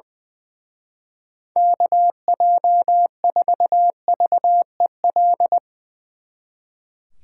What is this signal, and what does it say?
Morse code, heard as a keyed tone, about 22 wpm, 700 Hz. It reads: KJ4VEL
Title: Speed Racing - USA 2x3 Call Signs 20wpm